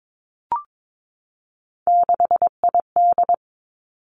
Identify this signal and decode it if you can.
Morse code, heard as a keyed tone, about 22 wpm, 700 Hz. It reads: 6ID
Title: Three Challenging Letter-Number Combinations 22wpm